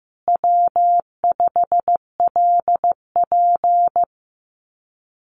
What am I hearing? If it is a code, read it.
Morse code, heard as a keyed tone, about 15 wpm, 700 Hz. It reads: W5LP
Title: Common QSO Phrases Encore 15wpm